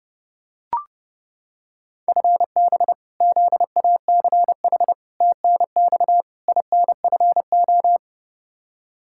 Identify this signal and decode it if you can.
Morse code, heard as a keyed tone, about 30 wpm, 700 Hz. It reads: FB ZACH TNX INFO
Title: K1USN SST QSO Practice 30wpm